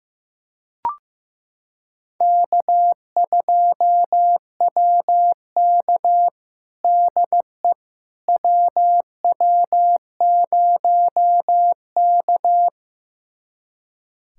Morse code, heard as a keyed tone, about 16 wpm, 700 Hz. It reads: K2WK DE WW0K
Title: USA Callsign DE Callsign - 4 Characters - 15wpm